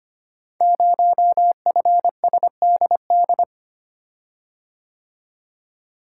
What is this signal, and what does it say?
Morse code, heard as a keyed tone, about 25 wpm, 700 Hz. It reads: 0FSDD
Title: Code Groups - 5 Letter:Numbers - 25wpm